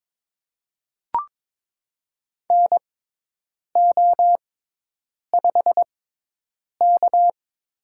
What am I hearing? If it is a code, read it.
Morse code, heard as a keyed tone, about 22 wpm, spaced as for 8 wpm, 700 Hz. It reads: NO5K
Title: USA Call Signs - 4 Characters - 6x Character Spacing - 22wpm